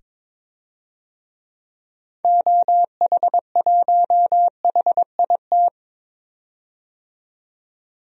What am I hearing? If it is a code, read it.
Morse code, heard as a keyed tone, about 21 wpm, 700 Hz. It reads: OH1HIT
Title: DXCC-100 Call Signs - 6 Characters - 22wpm